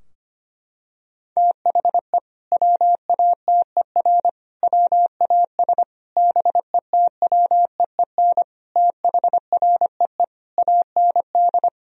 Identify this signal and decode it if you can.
Morse code, heard as a keyed tone, about 25 wpm, 700 Hz. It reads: THE WATER WAS BETWEEN THREE AND
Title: Sentences from Top 1000 Words - Review Part 3 - 25wpm